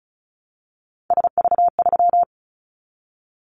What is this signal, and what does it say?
Morse code, heard as a keyed tone, about 36 wpm, 690 Hz. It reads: SV3
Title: Callsign Prefixes for Top 50 DXCC - 35wpm